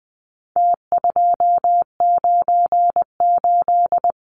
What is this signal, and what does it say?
Morse code, heard as a keyed tone, about 20 wpm, 700 Hz. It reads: T298
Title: Numbers - Contest Style - 4-Digit 20wpm